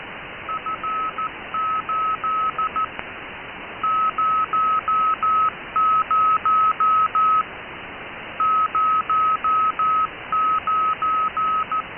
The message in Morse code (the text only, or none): F8 00 09